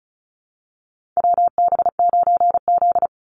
Morse code, heard as a keyed tone, about 34 wpm, 700 Hz. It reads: WB9Z
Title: Call Signs Repeated 2x - 35wpm